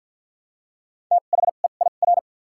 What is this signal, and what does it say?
Morse code, heard as a keyed tone, about 54 wpm, spaced as for 37 wpm, 700 Hz. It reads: THEIR
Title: 2x Character Spacing - Top 200 Words - 50wpm